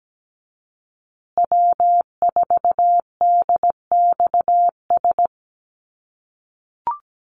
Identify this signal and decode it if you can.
Morse code, heard as a keyed tone, about 17 wpm, 700 Hz. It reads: W4DXS
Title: USA Call Signs Repeated 2x 17wpm